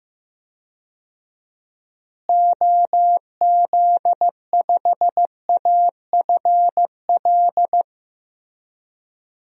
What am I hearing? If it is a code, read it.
Morse code, heard as a keyed tone, about 15 wpm, 700 Hz. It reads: OZ5AFL
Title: DXCC-100 Call Signs - 6 Characters - Repeated 3x - 15wpm